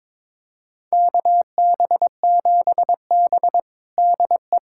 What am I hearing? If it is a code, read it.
Morse code, heard as a keyed tone, about 22 wpm, 700 Hz. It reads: KB7B DE
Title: USA Callsign DE Callsign - 4 Characters - 22wpm